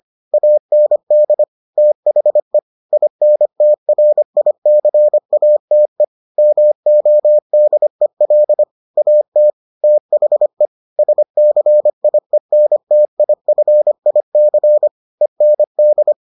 Morse code, heard as a keyed tone, about 25 wpm, 600 Hz. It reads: AND THE INTRICATE MODEL AT THE SCIENTIFIC END